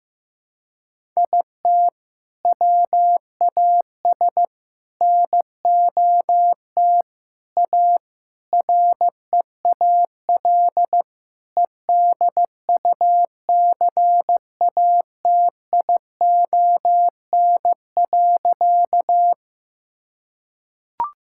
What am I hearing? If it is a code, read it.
Morse code, heard as a keyed tone, about 15 wpm, 700 Hz. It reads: IT WAS NOT A REAL EDUCATION.
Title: Sentences from Top 1000 Words - Review Part 6 - 15wpm